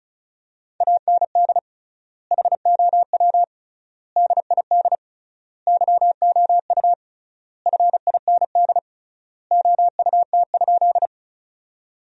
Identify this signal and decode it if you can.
Morse code, heard as a keyed tone, about 36 wpm, 700 Hz. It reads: AND HOW DID YOU FIND OUT?
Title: 3x Word-Spacing - Sentences from Top 100 Words - 35wpm